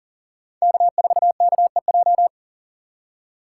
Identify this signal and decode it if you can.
Morse code, heard as a keyed tone, about 40 wpm, 700 Hz. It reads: K4KEJ
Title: USA Call Signs - 5 Characters - 2x Character Spacing - 40wpm